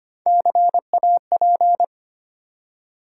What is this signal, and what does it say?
Morse code, heard as a keyed tone, about 25 wpm, 700 Hz. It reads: CAP
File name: practice / 25wpm / Easy Words with Hints - 25wpm